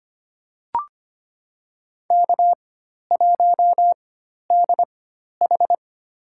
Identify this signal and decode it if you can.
Morse code, heard as a keyed tone, about 25 wpm, spaced as for 12 wpm, 700 Hz. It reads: K1DH
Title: USA Call Signs - 4 Characters - 4x Character Spacing - 25wpm